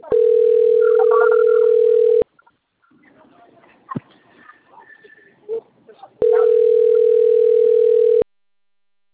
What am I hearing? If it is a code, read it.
Ringback tone (North America), 440 + 480 Hz.